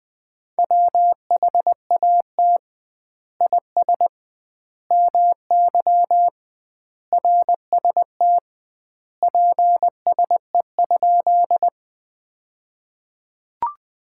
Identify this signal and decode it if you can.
Morse code, heard as a keyed tone, about 21 wpm, 700 Hz. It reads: WHAT IS MY RST PSE?
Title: Common QSO Phrases - 2x Word-Spacing - 20wpm